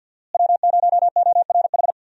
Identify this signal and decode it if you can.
Morse code, heard as a keyed tone, about 53 wpm, 700 Hz. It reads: W0ORH
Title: USA Call Signs - 5 Characters - Repeated 3x - 50wpm